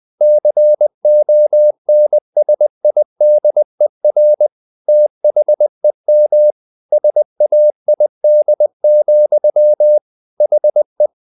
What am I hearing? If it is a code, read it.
Morse code, heard as a keyed tone, about 20 wpm, 600 Hz. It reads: CONSIDER THEM SAID, HE
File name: the old man and the sea part 2 20wpm